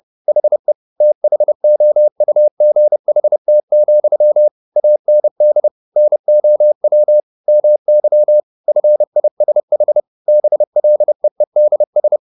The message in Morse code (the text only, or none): HE THOUGHT, AND NOW MY FISH BLEEDS